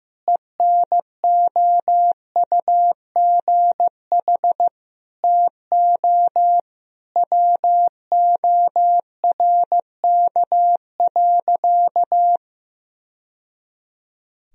Morse code, heard as a keyed tone, about 15 wpm, 700 Hz. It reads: ENOUGH TO WORK.
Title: Sentences from Top 1000 Words - Review Part 1 - 15wpm